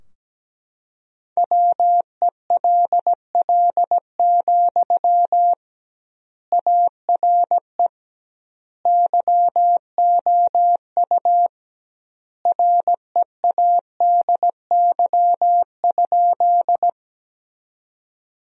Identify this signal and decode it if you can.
Morse code, heard as a keyed tone, about 17 wpm, 700 Hz. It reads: WELL, ARE YOU READY?
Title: 2x Word-Spacing - Sentences from Top 400 Words - 17wpm